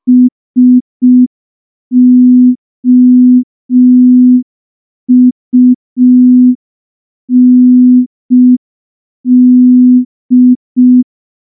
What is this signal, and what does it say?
Morse code, heard as a keyed tone, about 5 wpm, 250 Hz. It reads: SOUND
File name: morse / morse-training-05WPM